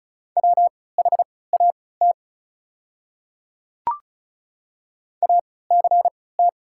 Morse code, heard as a keyed tone, about 36 wpm, spaced as for 20 wpm, 700 Hz. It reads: WHAT ACT
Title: 3x Character Spacing - Top 200 Words - 35wpm